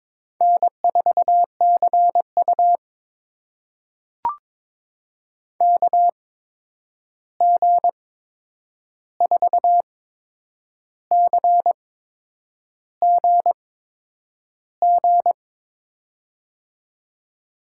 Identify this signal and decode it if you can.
Morse code, heard as a keyed tone, about 22 wpm, spaced as for 8 wpm, 700 Hz. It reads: N4CU KG4CGG
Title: USA Call Signs - 8x Character Spacing - 22wpm